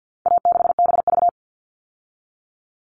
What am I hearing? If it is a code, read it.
Morse code, heard as a keyed tone, about 50 wpm, 700 Hz. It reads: A6BV
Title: All Cut Numbers - 4-Digit 50wpm